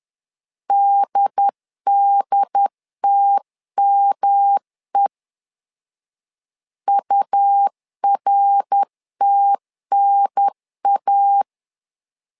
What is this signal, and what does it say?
Morse code, heard as a keyed tone, about 10 wpm, 800 Hz. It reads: DDTME URTNA
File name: LEZIONI / 3 MWRO / 3 MWR 10wpm 1952